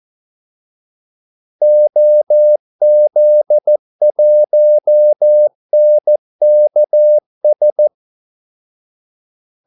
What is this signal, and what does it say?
Morse code, heard as a keyed tone, about 14 wpm, 600 Hz. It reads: OZ1NKS
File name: Callsigns-14wpm